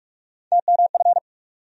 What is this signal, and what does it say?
Morse code, heard as a keyed tone, about 47 wpm, 700 Hz. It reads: TMF
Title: USA Call Sign Postfixes - 3 Letters 45wpm